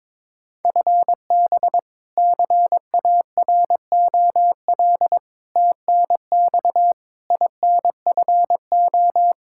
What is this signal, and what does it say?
Morse code, heard as a keyed tone, about 22 wpm, 700 Hz. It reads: FB CAROL TNX INFO
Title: Practice QSO - Exchange 3 - More Info - 22wpm